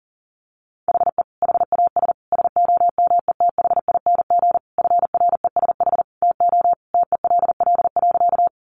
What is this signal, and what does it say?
Morse code, heard as a keyed tone, about 40 wpm, 700 Hz. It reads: HE HAS SOMETHING FRESH TO TELL.
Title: Sentences from Top 1200 Words 40wpm